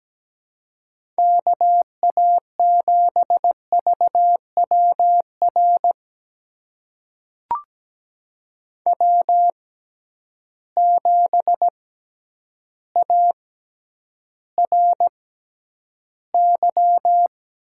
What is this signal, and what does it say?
Morse code, heard as a keyed tone, about 17 wpm, spaced as for 9 wpm, 700 Hz. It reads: KA7VWR W7ARY
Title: USA Call Signs - 6x Character Spacing - 17wpm